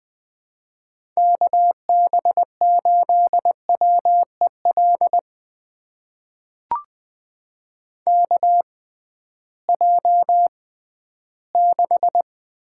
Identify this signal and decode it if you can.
Morse code, heard as a keyed tone, about 20 wpm, spaced as for 13 wpm, 700 Hz. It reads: KB8WEL KJ6
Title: USA Call Signs - 6 Characters - 6x Character Spacing - 20wpm